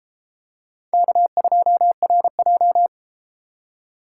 Morse code, heard as a keyed tone, about 32 wpm, 700 Hz. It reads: K2RJ